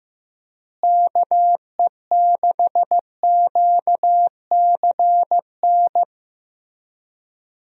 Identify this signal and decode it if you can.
Morse code, heard as a keyed tone, about 15 wpm, 700 Hz. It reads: KE6QCN
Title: USA Call Signs - 6 Characters - Repeated 2x - 15wpm